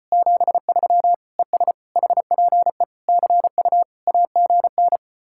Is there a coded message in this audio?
Yes: Morse code from 0.1 to 5.0 s, heard as a keyed tone, about 33 wpm, 700 Hz. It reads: 73 ES HPE CU AGN